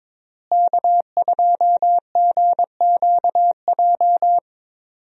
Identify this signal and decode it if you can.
Morse code, heard as a keyed tone, about 22 wpm, 700 Hz. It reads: K2GQJ
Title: USA Call Signs - 5 Characters - Repeated 2x - 22wpm